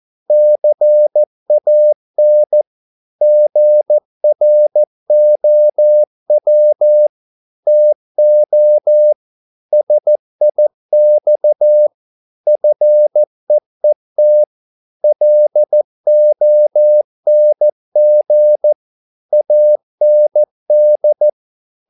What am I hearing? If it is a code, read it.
Morse code, heard as a keyed tone, about 14 wpm, 600 Hz. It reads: CAN GROW TO SIX FEET LONG AND